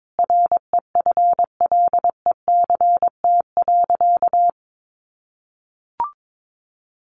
Morse code, heard as a keyed tone, about 22 wpm, 700 Hz. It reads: REFLECT.
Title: Sentences from Top 1500 Words 22wpm